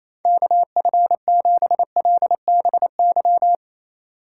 Morse code, heard as a keyed tone, about 27 wpm, 700 Hz. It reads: KF7LBY
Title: USA Call Signs - 6 Characters - 3x Character Spacing - 28wpm